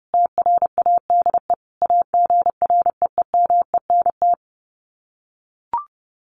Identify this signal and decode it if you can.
Morse code, heard as a keyed tone, about 28 wpm, 700 Hz. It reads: TRADE AGREEMENT
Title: Sets of 5 Words Encore 30wpm